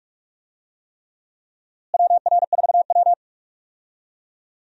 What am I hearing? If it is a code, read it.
Morse code, heard as a keyed tone, about 46 wpm, 700 Hz. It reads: WR4W
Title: USA Call Signs - 4 Characters - Repeated 3x - 45wpm